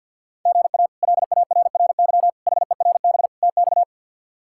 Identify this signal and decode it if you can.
Morse code, heard as a keyed tone, about 48 wpm, 700 Hz. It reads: GA LARRY HERB TX
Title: K1USN SST QSO Practice - 50wpm